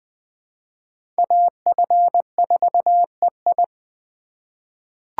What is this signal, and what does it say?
Morse code, heard as a keyed tone, about 20 wpm, 700 Hz. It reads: AF4EI
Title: USA Call Signs - 5 Characters - 3x Character Spacing - 20wpm